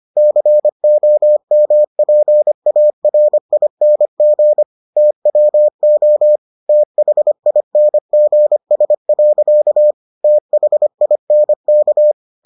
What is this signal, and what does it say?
Morse code, heard as a keyed tone, about 25 wpm, 600 Hz. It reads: COMPARING TWO THINGS. THINK